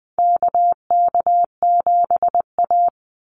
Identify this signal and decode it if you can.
Morse code, heard as a keyed tone, about 20 wpm, 700 Hz. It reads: KK7A
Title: Call Signs Repeated 2x - 20wpm